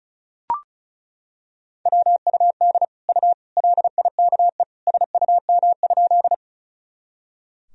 Morse code, heard as a keyed tone, about 34 wpm, 700 Hz. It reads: WUD U LIKE SUM?
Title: Sentences from Top 100 Words and Abbreviations 35wpm